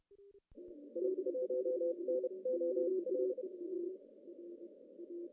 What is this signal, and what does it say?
Morse code, heard as a keyed tone, about 30 wpm, spaced as for 25 wpm, 510 Hz. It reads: I1NOL
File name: CW / cqww-cw-k5tr